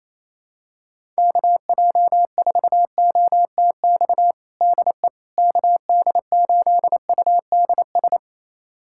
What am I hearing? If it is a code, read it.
Morse code, heard as a keyed tone, about 28 wpm, 700 Hz. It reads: KJ4OTX DE KD8UDS
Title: USA Callsign DE Callsign - 6 Characters - 28wpm